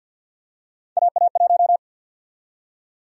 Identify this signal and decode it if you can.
Morse code, heard as a keyed tone, about 48 wpm, 700 Hz. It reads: AA1
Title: USA Call Sign Prefixes - 50wpm